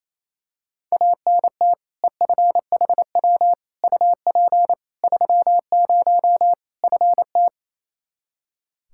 Morse code, heard as a keyed tone, about 28 wpm, 700 Hz. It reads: ANT EFHW UP 30 FT
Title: Practice QSO - Exchange 2 - Info - 28wpm